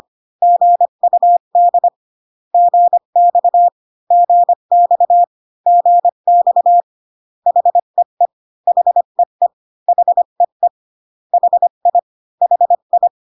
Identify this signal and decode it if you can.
Morse code, heard as a keyed tone, about 25 wpm, 700 Hz. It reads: GUD GX GX GX HEE HEE HEE HI HI